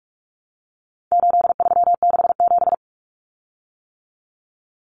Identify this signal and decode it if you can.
Morse code, heard as a keyed tone, about 44 wpm, 700 Hz. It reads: 8367
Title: Common Cut Numbers - 4-Digit 45wpm